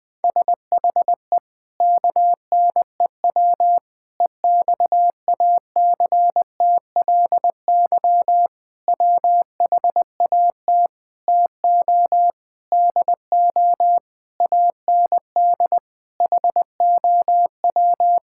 Morse code, heard as a keyed tone, about 21 wpm, 700 Hz. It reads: SHE KNEW EXACTLY WHAT TO DO AND HOW
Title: Sentences with 8-Word Phrases - Spoken Beginning - 20wpm